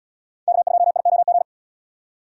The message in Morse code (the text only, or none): K2IOG